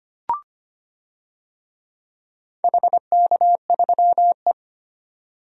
Morse code, heard as a keyed tone, about 25 wpm, 700 Hz. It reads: HK3E
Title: DXCC-100 Call Signs - 4 Characters - 25wpm